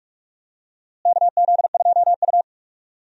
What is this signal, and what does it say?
Morse code, heard as a keyed tone, about 42 wpm, 700 Hz. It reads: KZ2U
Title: USA Call Signs - 4 Characters - Repeated 2x - 45wpm